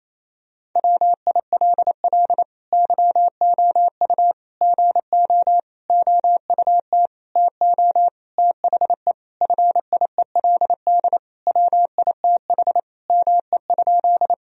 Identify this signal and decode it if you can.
Morse code, heard as a keyed tone, about 28 wpm, 700 Hz. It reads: WILL YOU GO OUT TO THE FIELD WITH ME?
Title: Sentences from Top 1000 Words - Review Part 2 - 28wpm